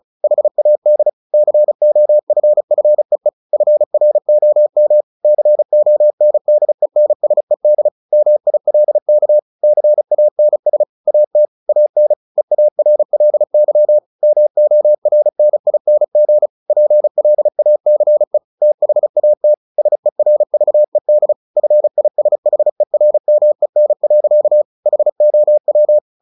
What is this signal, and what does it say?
Morse code, heard as a keyed tone, about 34 wpm, 600 Hz. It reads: HAD COFFEE FROM CONDENSED MILK CANS AT AN EARLY MORNING PLACE THAT SERVED FISHERMEN. HOW
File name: the old man and the sea / the old man and the sea 1 35wpm